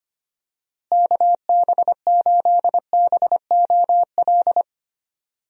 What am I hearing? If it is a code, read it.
Morse code, heard as a keyed tone, about 25 wpm, 700 Hz. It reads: KB8BOL
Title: USA Call Signs - 6 Characters - Repeated 2x - 25wpm